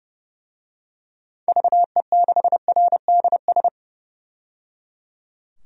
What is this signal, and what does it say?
Morse code, heard as a keyed tone, about 30 wpm, 700 Hz. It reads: VE6RDS
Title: DXCC-100 Call Signs - 6 Characters - 30wpm